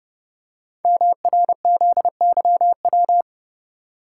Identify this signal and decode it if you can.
Morse code, heard as a keyed tone, about 28 wpm, 700 Hz. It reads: MRZYW
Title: Code Groups - 5 Letters - 30wpm